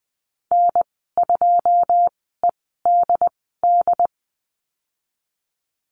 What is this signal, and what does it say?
Morse code, heard as a keyed tone, about 20 wpm, spaced as for 15 wpm, 700 Hz. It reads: N2EDD
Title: 128 Lesson 21g - Number 6 - Call Signs 20wpm